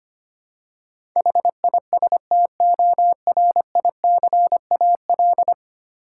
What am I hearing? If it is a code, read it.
Morse code, heard as a keyed tone, about 25 wpm, 700 Hz. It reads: HISTORICAL
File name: Ten Letter Words - 25wpm